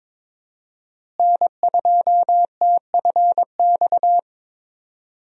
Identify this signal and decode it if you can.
Morse code, heard as a keyed tone, about 22 wpm, 700 Hz. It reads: N2TFX
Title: USA Call Signs - 5 Characters - Repeated 2x - 22wpm